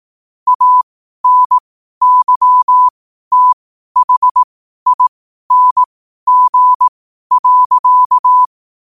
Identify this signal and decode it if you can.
Morse code, heard as a keyed tone, about 18 wpm, spaced as for 12 wpm, 1000 Hz. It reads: ANYTHING.
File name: qotdfiles / 12wpm